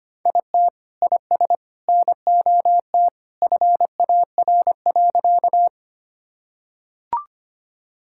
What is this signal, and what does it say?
Morse code, heard as a keyed tone, about 25 wpm, 700 Hz. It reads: IT IS NOT FAR.
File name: practice / 25wpm / Sentences with 3-Word Phrases - 25wpm